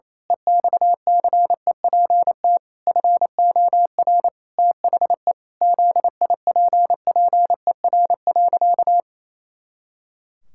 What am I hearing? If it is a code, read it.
Morse code, heard as a keyed tone, about 28 wpm, 700 Hz. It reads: EXCEPT FOR THE ZIPPER.
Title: Pangram Challenge v2 - 28wpm